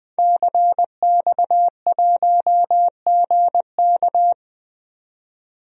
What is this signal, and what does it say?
Morse code, heard as a keyed tone, about 20 wpm, 700 Hz. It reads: CX1GK
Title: DXCC-100 Call Signs - 5 Characters - 20wpm